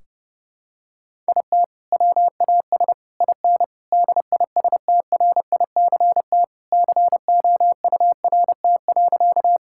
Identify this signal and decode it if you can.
Morse code, heard as a keyed tone, about 30 wpm, 700 Hz. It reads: IT WAS IN DISTRICT COURT.
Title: Sentences from Top 1600 Words 30wpm